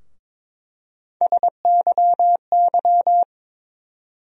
Morse code, heard as a keyed tone, about 22 wpm, 700 Hz. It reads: SYY
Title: USA Call Sign Postfixes - 3 Letters 22wpm